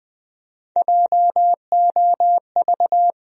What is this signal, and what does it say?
Morse code, heard as a keyed tone, about 20 wpm, 700 Hz. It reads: JOV
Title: USA Call Sign Postfixes - 3 Letters 20wpm